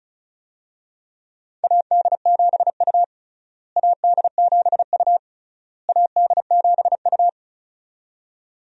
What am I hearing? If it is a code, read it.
Morse code, heard as a keyed tone, about 36 wpm, 700 Hz. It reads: AD7U AD7U AD7U
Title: USA Call Signs - 4 Characters - Repeated 3x - 35wpm